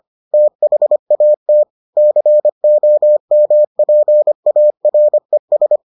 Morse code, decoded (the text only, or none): THAT COMPARES